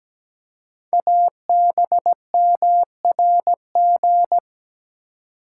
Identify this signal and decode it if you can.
Morse code, heard as a keyed tone, about 17 wpm, 700 Hz. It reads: ABMRG
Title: Code Groups - 5 Letters - 17wpm